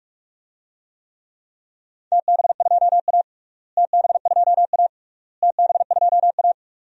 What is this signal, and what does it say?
Morse code, heard as a keyed tone, about 46 wpm, 700 Hz. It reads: TB2A TB2A TB2A
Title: DXCC-100 Call Signs - 4 Characters - Repeated 3x - 45wpm